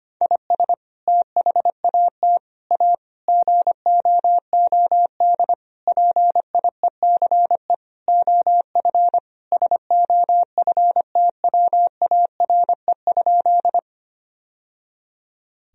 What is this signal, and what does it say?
Morse code, heard as a keyed tone, about 25 wpm, 700 Hz. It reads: IS THAT A GOOD PIECE OF SOFTWARE?
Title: Sentences from Top 1000 Words - Review Part 3 - 25wpm